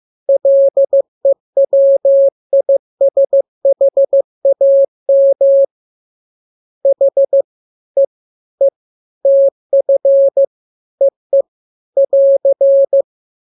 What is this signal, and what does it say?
Morse code, heard as a keyed tone, about 15 wpm, 550 Hz. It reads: LEWISHAM H E E TF EE <AR>
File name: jokes / joke 15 15 0000